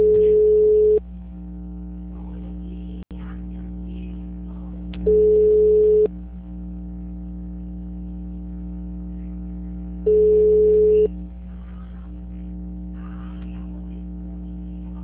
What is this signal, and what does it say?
Ringback tone (Europe, 425 Hz).